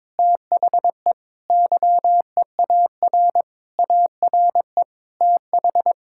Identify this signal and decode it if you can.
Morse code, heard as a keyed tone, about 22 wpm, 700 Hz. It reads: THE YEAR ARE TH
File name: Sentences with 7-Word Phrases - 22wpm